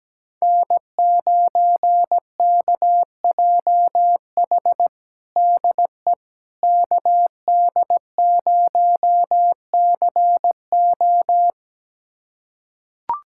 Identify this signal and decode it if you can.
Morse code, heard as a keyed tone, about 17 wpm, 700 Hz. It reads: N9KJH DE KD0CO
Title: USA Callsign DE Callsign - 5 Characters - 17wpm